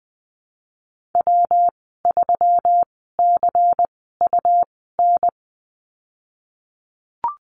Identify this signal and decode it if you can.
Morse code, heard as a keyed tone, about 20 wpm, spaced as for 15 wpm, 700 Hz. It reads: W3CUN
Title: 193 Lesson 31g - Letter v - Call Signs 20wpm